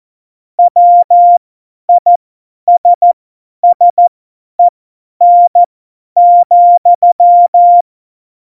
Morse code, heard as a keyed tone, about 14 wpm, spaced as for 10 wpm, 700 Hz. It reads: WISSEN,
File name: Morsekurs / Witze / Witz 22. Oktober 2019, 50 F70